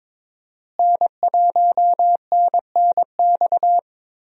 Morse code, heard as a keyed tone, about 22 wpm, 700 Hz. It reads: N1NNX
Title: USA Call Signs - 5 Characters - Repeated 3x - 22wpm